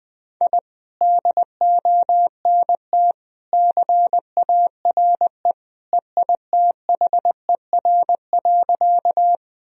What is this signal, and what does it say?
Morse code, heard as a keyed tone, about 20 wpm, 700 Hz. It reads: I DONT CARE EITHER.
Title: Sentences from Top 1000 Words - Review Part 4 - 20wpm